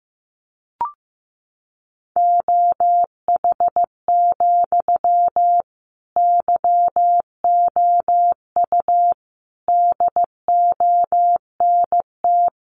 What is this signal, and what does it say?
Morse code, heard as a keyed tone, about 15 wpm, 700 Hz. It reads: OH, YOU DONT
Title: Sentences from Top 1000 Words 15wpm